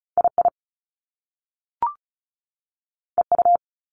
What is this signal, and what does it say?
Morse code, heard as a keyed tone, about 36 wpm, 700 Hz. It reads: II EU
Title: Top Two Letter Combinations in Words 35wpm